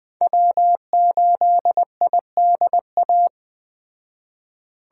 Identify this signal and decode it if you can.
Morse code, heard as a keyed tone, about 20 wpm, 700 Hz. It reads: W8IDA